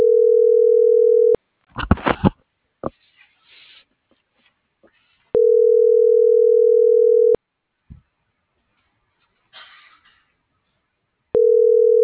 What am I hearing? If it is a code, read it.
Ringback tone (North America), 440 + 480 Hz.
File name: CENTRAL SOTOVA / out alejandro.rodriguez - .50184